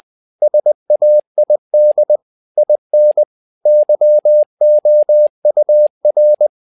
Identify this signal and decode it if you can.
Morse code, heard as a keyed tone, about 20 wpm, 600 Hz. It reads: SAID IN YOUR
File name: tower trouble 20wpm